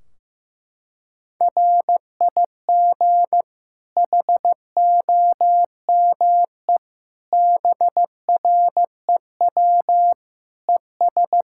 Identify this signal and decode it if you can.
Morse code, heard as a keyed tone, about 15 wpm, 700 Hz. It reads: RIG HOME BREW ES
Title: Practice QSO - Exchange 2 - Info - 15wpm